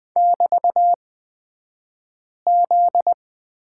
Morse code, heard as a keyed tone, about 20 wpm, spaced as for 5 wpm, 700 Hz. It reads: <BT>Z
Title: 242 Lesson 39d - Letter z - Two Characters 20wpm